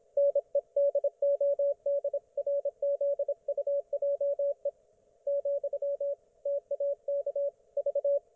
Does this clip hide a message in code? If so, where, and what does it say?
Yes: Morse code from 0.2 to 8.2 s, heard as a keyed tone, about 26 wpm, 560 Hz, with radio noise. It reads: NEDODRZUJE , TAK V